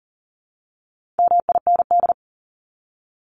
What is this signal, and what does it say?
Morse code, heard as a keyed tone, about 40 wpm, 700 Hz. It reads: MIND
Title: Top 500 Words 40wpm